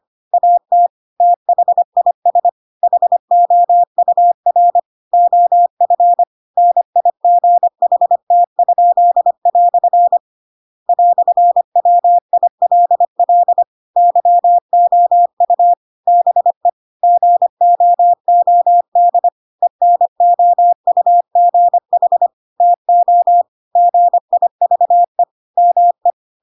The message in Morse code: AT THIS HOUR OF NIGHT?" "WILL YOU BE GOOD ENOUGH TO GIVE ME

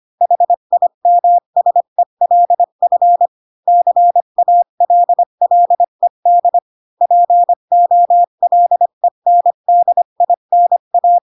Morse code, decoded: HIMSELF CALLED POLENDINA

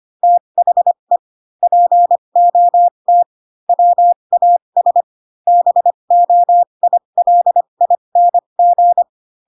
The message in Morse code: THE POT WAS BOILING

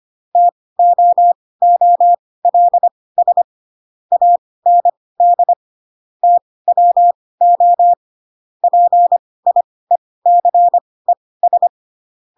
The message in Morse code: TOOLS AND TWO PIECES

